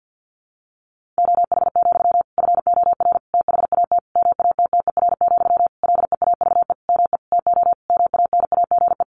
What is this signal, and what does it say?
Morse code, heard as a keyed tone, about 50 wpm, 700 Hz. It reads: OH, FOR THAT MATTER, LEAVE ME TO MANAGE
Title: Sentences from Top 1400 Words 50wpm